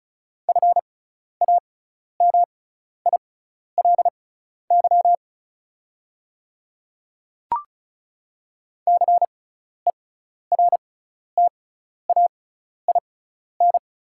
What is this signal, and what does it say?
Morse code, heard as a keyed tone, about 36 wpm, spaced as for 12 wpm, 700 Hz. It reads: FAMILY CERTAIN
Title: 6x Character Spacing - Top 500 Words - 35wpm